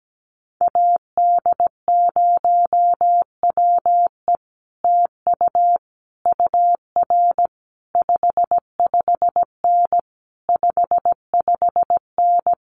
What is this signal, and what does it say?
Morse code, heard as a keyed tone, about 17 wpm, 700 Hz. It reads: AD0WE TU UR 55N 55N
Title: Common QSO Phrases Encore 17wpm